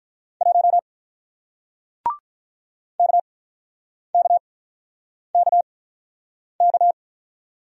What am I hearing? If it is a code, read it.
Morse code, heard as a keyed tone, about 48 wpm, spaced as for 9 wpm, 700 Hz. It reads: 1 KKKK